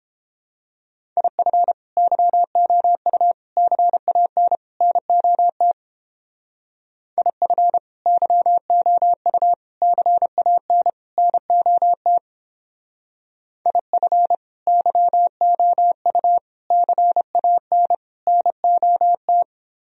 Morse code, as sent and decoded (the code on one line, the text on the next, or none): .. ..-. / -.-- --- ..- / -.-. .- -. / -. --- - / .. ..-. / -.-- --- ..- / -.-. .- -. / -. --- - / .. ..-. / -.-- --- ..- / -.-. .- -. / -. --- -
IF YOU CAN NOT IF YOU CAN NOT IF YOU CAN NOT